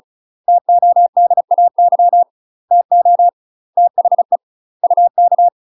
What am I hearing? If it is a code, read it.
Morse code, heard as a keyed tone, about 34 wpm, 700 Hz. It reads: TODAY TO THE UK
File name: UK MASTER QSOs TRAINING/35WPM